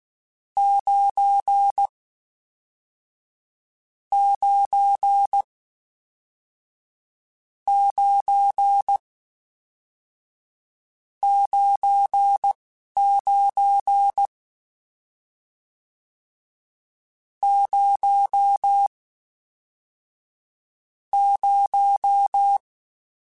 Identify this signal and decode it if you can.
Morse code, heard as a keyed tone, about 16 wpm, 800 Hz. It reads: 9 9 9 9 9 0 0